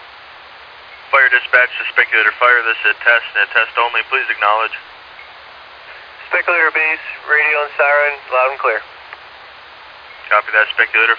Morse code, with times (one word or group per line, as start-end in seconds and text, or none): none